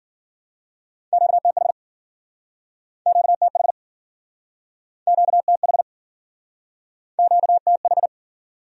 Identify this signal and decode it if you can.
Morse code, heard as a keyed tone, about 53 wpm, 700 Hz. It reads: QTH QTH QTH QTH